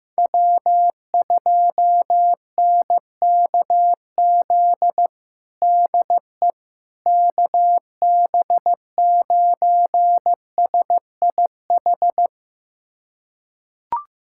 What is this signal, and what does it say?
Morse code, heard as a keyed tone, about 15 wpm, 700 Hz. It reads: W2NKZ DE KB9SIH
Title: USA Callsign DE Callsign 15wpm